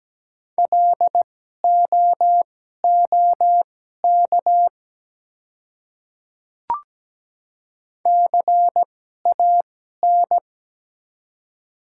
Morse code, heard as a keyed tone, about 17 wpm, spaced as for 12 wpm, 700 Hz. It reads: LOOK CAN
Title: 2x Character Spacing - Top 100 Words - 17wpm